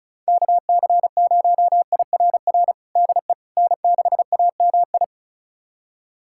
Morse code, heard as a keyed tone, about 36 wpm, 700 Hz. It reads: KC0IRR DE N6AMI
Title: USA Callsign DE Callsign 35wpm